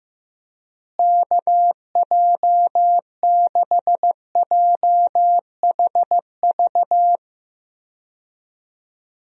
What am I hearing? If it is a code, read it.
Morse code, heard as a keyed tone, about 15 wpm, 700 Hz. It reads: KJ6JHV
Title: USA Call Signs - 6 Characters - Repeated 2x - 15wpm